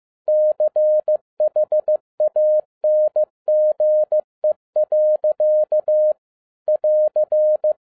Morse code, heard as a keyed tone, about 15 wpm, 610 Hz. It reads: CHANGE. <AR>